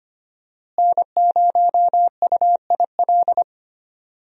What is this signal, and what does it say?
Morse code, heard as a keyed tone, about 25 wpm, 700 Hz. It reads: N0UIL
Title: USA Call Signs - 5 Characters - Repeated 2x - 25wpm